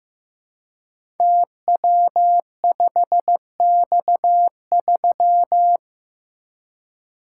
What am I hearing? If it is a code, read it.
Morse code, heard as a keyed tone, about 15 wpm, 700 Hz. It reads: TW5X3
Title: Code Groups - 5 Letter:Numbers - 15wpm